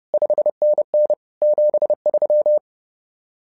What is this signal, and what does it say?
Morse code, heard as a keyed tone, about 30 wpm, 600 Hz. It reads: 5NN 73
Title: SOTA QSO Practice 30wpm